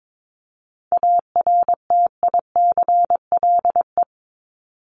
Morse code, heard as a keyed tone, about 22 wpm, 700 Hz. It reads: ARTICLE
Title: Top 751-1000 Words 22wpm